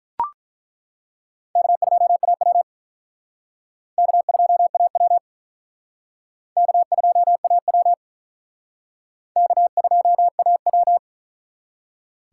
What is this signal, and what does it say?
Morse code, heard as a keyed tone, about 42 wpm, 700 Hz. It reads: K2AW K2AW K2AW K2AW
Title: Speed Racing - USA Call Signs 35wpm